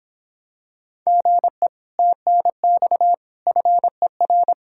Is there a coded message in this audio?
Yes: Morse code from 1.1 to 4.5 s, heard as a keyed tone, about 26 wpm, 700 Hz. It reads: GE TNX FER